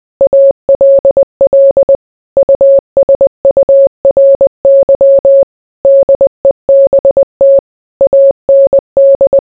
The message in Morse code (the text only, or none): ALL USURY DEBT AND